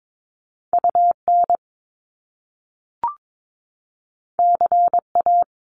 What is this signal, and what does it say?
Morse code, heard as a keyed tone, about 22 wpm, 700 Hz. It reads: UN CA
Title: Top Two Letter Combinations in Words 22wpm